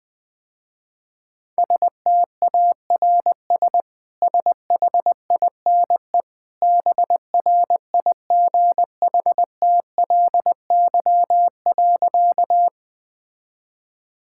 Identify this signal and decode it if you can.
Morse code, heard as a keyed tone, about 21 wpm, 700 Hz. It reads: STARS SHINE BRIGHTLY.
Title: Easy Sentences with Hints - 20wpm